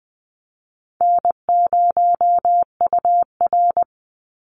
Morse code, heard as a keyed tone, about 20 wpm, 700 Hz. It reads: N0UR
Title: Easy Call Signs Repeated 2x - 20wpm